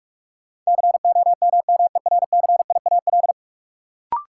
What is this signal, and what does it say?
Morse code, heard as a keyed tone, about 46 wpm, 700 Hz. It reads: COMMERCIAL
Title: Sentences from Top 1000 Words - Review Part 5 - New Word Drill - 45wpm